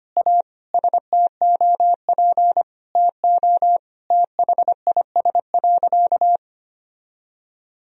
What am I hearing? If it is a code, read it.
Morse code, heard as a keyed tone, about 25 wpm, 700 Hz. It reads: A STOP TO THIS.
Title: Sentences from Top 1000 Words - Review Part 1 - 25wpm